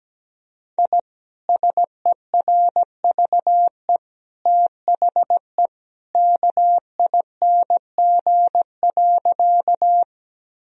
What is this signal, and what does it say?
Morse code, heard as a keyed tone, about 17 wpm, 700 Hz. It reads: I SERVE THE KING.
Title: Sentences from Top 1000 Words - Review Part 2 - 17wpm